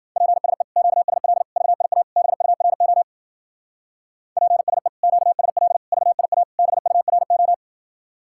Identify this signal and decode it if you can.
Morse code, heard as a keyed tone, about 50 wpm, 700 Hz. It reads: PSE QSL VIA BURO PSE QSL VIA BURO